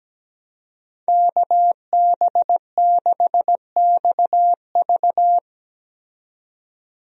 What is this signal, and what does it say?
Morse code, heard as a keyed tone, about 17 wpm, 700 Hz. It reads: KB6XV
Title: USA Call Signs - 5 Characters - 17wpm